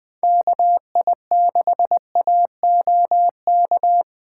Morse code, heard as a keyed tone, about 23 wpm, 700 Hz. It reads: KI6AOK
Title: Speed Racing - USA 2x3 Call Signs 20wpm